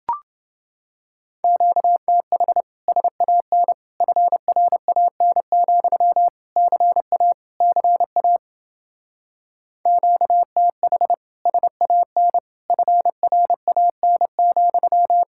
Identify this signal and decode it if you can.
Morse code, heard as a keyed tone, about 30 wpm, 700 Hz. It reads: QTH SAN FRAN, CA CA QTH SAN FRAN,